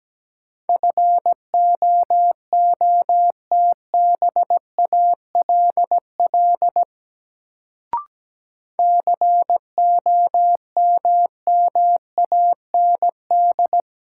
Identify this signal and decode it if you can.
Morse code, heard as a keyed tone, about 17 wpm, 700 Hz. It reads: FOOTBALL COMMAND
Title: Sentences from Top 1600 Words - New Word Drill - 17wpm